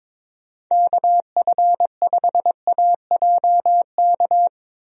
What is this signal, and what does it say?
Morse code, heard as a keyed tone, about 22 wpm, 700 Hz. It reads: KF5AJK
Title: USA Call Signs - 6 Characters - Repeated 2x - 22wpm